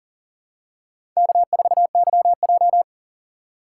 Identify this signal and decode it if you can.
Morse code, heard as a keyed tone, about 38 wpm, 700 Hz. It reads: K4YJ
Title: USA Call Signs - 4 Characters - 4x Character Spacing - 40wpm